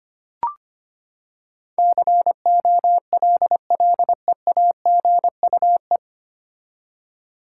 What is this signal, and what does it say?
Morse code, heard as a keyed tone, about 25 wpm, 700 Hz. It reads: COLLEAGUE
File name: Nine Letter Words - 25wpm